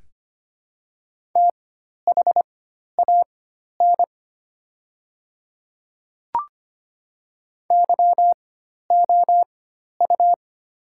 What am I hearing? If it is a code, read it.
Morse code, heard as a keyed tone, about 25 wpm, spaced as for 12 wpm, 700 Hz. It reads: THAN YOU
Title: 4x Character Spacing - Top 300 Words - 25wpm